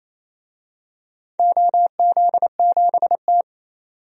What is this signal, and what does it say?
Morse code, heard as a keyed tone, about 28 wpm, 700 Hz. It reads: OZ7T
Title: DXCC-100 Call Signs - 4 Characters - 28wpm